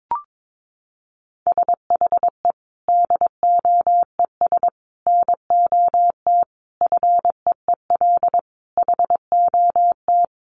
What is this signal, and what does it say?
Morse code, heard as a keyed tone, about 22 wpm, 700 Hz. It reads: SHE DOES NOT FEEL HOT
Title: Additional Easy Phrases 22wpm